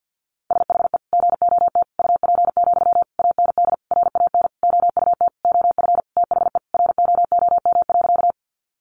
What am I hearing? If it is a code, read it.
Morse code, heard as a keyed tone, about 52 wpm, 700 Hz. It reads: SHE GOT UP, AND RAN OUT OF THE ROOM.
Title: Sentences from Top 500 Words 50wpm